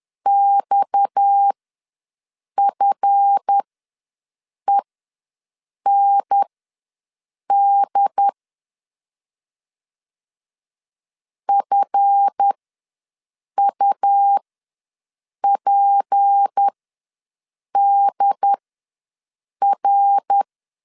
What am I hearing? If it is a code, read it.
Morse code, heard as a keyed tone, about 10 wpm, spaced as for 6 wpm, 800 Hz. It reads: XFEND FUPDR
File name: LEZIONI / 7 GF9Z / 7 GF ws 10wpm 3918